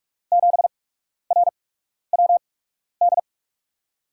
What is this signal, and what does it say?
Morse code, heard as a keyed tone, about 44 wpm, spaced as for 12 wpm, 700 Hz. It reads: 7RWD
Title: USA Call Signs - 6 Characters - 8x Character Spacing - 45wpm